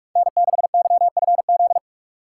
Morse code, heard as a keyed tone, about 46 wpm, 700 Hz. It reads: N6YFZ
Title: USA Call Signs - 3x Character Spacing - 45wpm